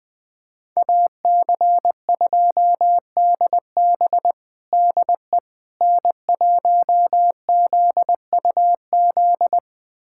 Morse code, heard as a keyed tone, about 20 wpm, 700 Hz. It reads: AC2DB DE N1ZUZ
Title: USA Callsign DE Callsign - 5 Characters - 20wpm